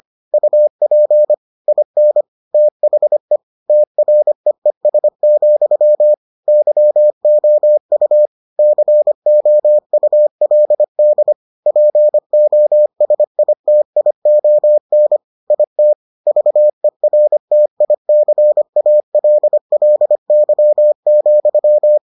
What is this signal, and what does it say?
Morse code, heard as a keyed tone, about 25 wpm, 600 Hz. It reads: UP IN THE TREES, YOU COULD POSITION IT VERTICALLY,